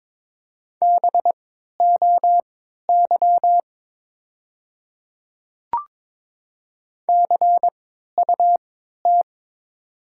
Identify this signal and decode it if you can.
Morse code, heard as a keyed tone, about 22 wpm, spaced as for 12 wpm, 700 Hz. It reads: BOY CUT
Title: 3x Character Spacing - Top 500 Words - 22wpm